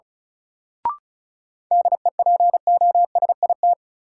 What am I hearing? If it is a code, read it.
Morse code, heard as a keyed tone, about 36 wpm, 700 Hz. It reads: DEPOSIT
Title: Sentences from Top 1300 Words - New Word Drill - 35wpm